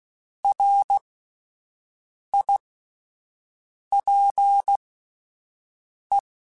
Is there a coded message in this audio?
Yes: Morse code from 0.4 to 6.2 s, heard as a keyed tone, about 16 wpm, spaced as for 5 wpm, 800 Hz. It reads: RIPE